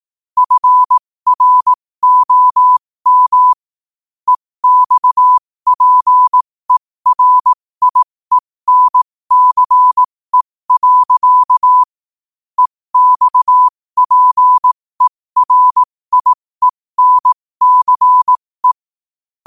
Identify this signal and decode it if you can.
Morse code, heard as a keyed tone, about 18 wpm, spaced as for 15 wpm, 1000 Hz. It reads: FROM EXPERIENCE. EXPERIENCE
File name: qotdfiles / 15wpm